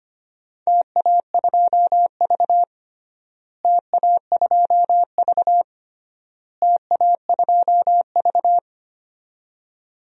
Morse code, heard as a keyed tone, about 25 wpm, 700 Hz. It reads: TA2V TA2V TA2V
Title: DXCC-100 Call Signs - 4 Characters - Repeated 3x - 25wpm